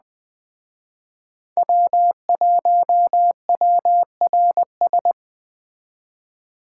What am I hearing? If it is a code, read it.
Morse code, heard as a keyed tone, about 20 wpm, 700 Hz. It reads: W1WRS
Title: Speed Racing - USA 1x3 Call Signs 17wpm